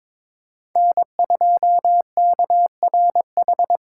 Morse code, heard as a keyed tone, about 21 wpm, 700 Hz. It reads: N2KRH
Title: USA Call Signs - 5 Characters - 2x Character Spacing - 22wpm